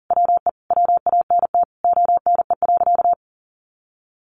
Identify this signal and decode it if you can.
Morse code, heard as a keyed tone, about 37 wpm, 700 Hz. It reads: WE WANT ONE.
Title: Sentences from Top 1200 Words 40wpm